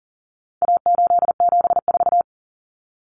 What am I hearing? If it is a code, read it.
Morse code, heard as a keyed tone, about 40 wpm, 700 Hz. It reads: A874
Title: Common Cut Numbers - 4-Digit 40wpm